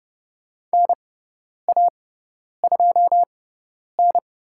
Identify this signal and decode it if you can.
Morse code, heard as a keyed tone, about 30 wpm, spaced as for 10 wpm, 700 Hz. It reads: NA2N
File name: Lesson 18g - Letter m - Call Signs 30wpm